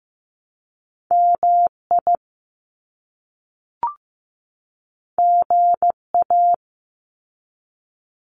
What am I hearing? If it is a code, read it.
Morse code, heard as a keyed tone, about 15 wpm, 700 Hz. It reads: MI GA
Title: US States 15wpm